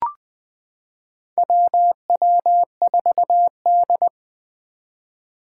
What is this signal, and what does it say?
Morse code, heard as a keyed tone, about 21 wpm, 700 Hz. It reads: WW4D
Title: 2023 SOTA Activator Call Signs - 20wpm